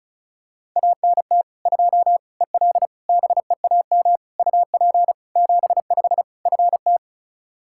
Morse code, heard as a keyed tone, about 34 wpm, 700 Hz. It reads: ANT 2 EL BEAM UP 75 FT
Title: Practice QSO - Exchange 2 - Info - 35wpm